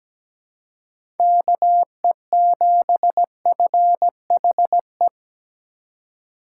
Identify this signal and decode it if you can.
Morse code, heard as a keyed tone, about 17 wpm, 700 Hz. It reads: KE7FHE
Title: USA Call Signs - 2x Character Spacing - 17wpm